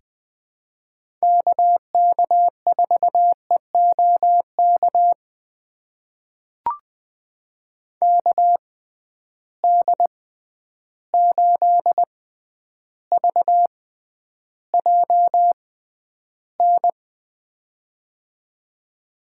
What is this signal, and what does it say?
Morse code, heard as a keyed tone, about 20 wpm, spaced as for 10 wpm, 700 Hz. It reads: KK4EOK KD8VJN
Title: USA Call Signs - 6 Characters - 6x Character Spacing - 20wpm